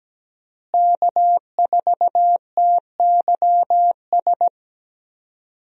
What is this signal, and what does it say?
Morse code, heard as a keyed tone, about 17 wpm, 700 Hz. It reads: K4TYS
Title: USA Call Signs - 5 Characters - Repeated 2x - 17wpm